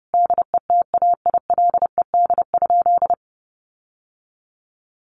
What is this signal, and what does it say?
Morse code, heard as a keyed tone, about 30 wpm, 700 Hz. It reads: DETAILED?
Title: Sentences from Top 1500 Words 30wpm